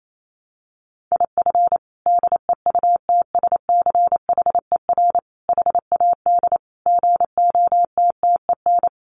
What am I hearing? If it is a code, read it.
Morse code, heard as a keyed tone, about 28 wpm, 700 Hz. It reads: IF DEUTSCHER HAD GOTTEN
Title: A Sound Of Thunder 28wpm